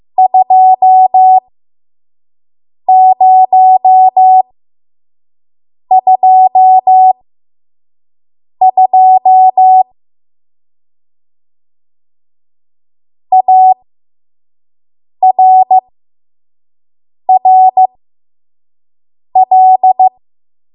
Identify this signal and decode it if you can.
Morse code, heard as a keyed tone, about 15 wpm, spaced as for 5 wpm, 750 Hz. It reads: 2022 ARRL
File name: Morse / 05WPM